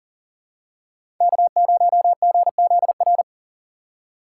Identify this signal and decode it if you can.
Morse code, heard as a keyed tone, about 40 wpm, 700 Hz. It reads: K0GZR
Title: USA Call Signs - 5 Characters - 4x Character Spacing - 40wpm